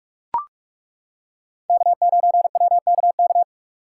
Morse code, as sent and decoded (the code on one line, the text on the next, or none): -.- ----. .-- -.- -.-
K9WKK